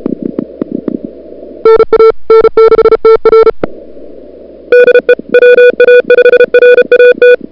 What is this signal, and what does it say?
Morse code, heard as a keyed tone, about 31 wpm, 440 Hz. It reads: NA N6TR DE WA5RAT
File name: CWSprint / February 2007 CW Sprint / 01-2007 CW Sprint Hour1